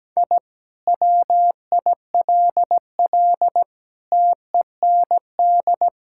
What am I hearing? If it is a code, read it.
Morse code, heard as a keyed tone, about 17 wpm, 700 Hz. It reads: I WILL TEND
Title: Sentences from Top 1000 Words - Review Part 5 - 17wpm